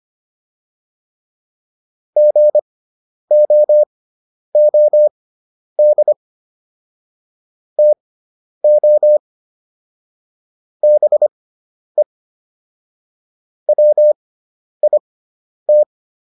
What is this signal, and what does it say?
Morse code, heard as a keyed tone, about 25 wpm, spaced as for 10 wpm, 600 Hz. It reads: GOOD TO BE WIT